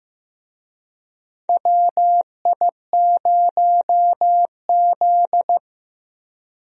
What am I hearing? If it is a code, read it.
Morse code, heard as a keyed tone, about 16 wpm, 700 Hz. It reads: WI0Z
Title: USA Call Signs - 4 Characters - Repeated 3x - 15wpm